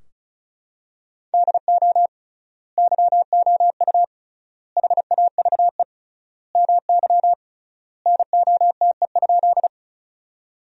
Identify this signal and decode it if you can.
Morse code, heard as a keyed tone, about 36 wpm, 700 Hz. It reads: DO YOU HAVE MY NOTE?
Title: 3x Word-Spacing - Sentences from Top 500 Words - 35wpm